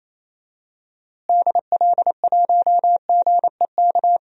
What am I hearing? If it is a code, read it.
Morse code, heard as a keyed tone, about 28 wpm, 700 Hz. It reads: DL1GEK
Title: DXCC-100 Call Signs - 6 Characters - Repeated 2x - 28wpm